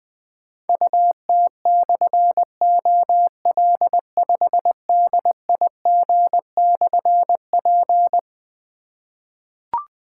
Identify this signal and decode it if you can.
Morse code, heard as a keyed tone, about 21 wpm, 700 Hz. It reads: UT/OL5DIG/P
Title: Difficult Call Signs 20wpm